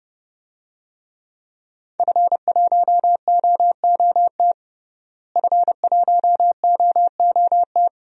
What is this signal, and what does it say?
Morse code, heard as a keyed tone, about 31 wpm, 700 Hz. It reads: F1OOT F1OOT
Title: DXCC-100 Call Signs - 5 Characters - Repeated 2x - 30wpm